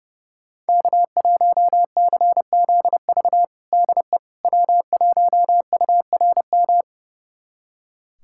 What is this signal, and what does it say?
Morse code, heard as a keyed tone, about 30 wpm, 700 Hz. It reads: K1CZV DE W1URM
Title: USA Callsign DE Callsign - 5 Characters - 30wpm